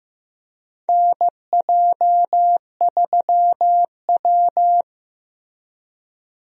Morse code, heard as a keyed tone, about 15 wpm, 700 Hz. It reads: NJ3W
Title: USA Call Signs - 4 Characters - Repeated 2x - 15wpm